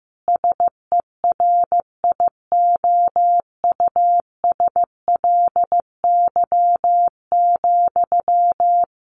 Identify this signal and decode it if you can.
Morse code, heard as a keyed tone, about 15 wpm, 700 Hz. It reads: SERIOUSLY,
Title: Sentences from Top 1200 Words 15wpm